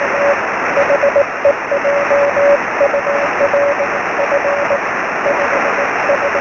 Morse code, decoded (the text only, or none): none